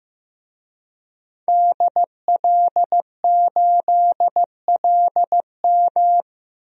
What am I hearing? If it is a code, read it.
Morse code, heard as a keyed tone, about 15 wpm, 700 Hz. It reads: DL8LM
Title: DXCC-100 Call Signs - 5 Characters - 15wpm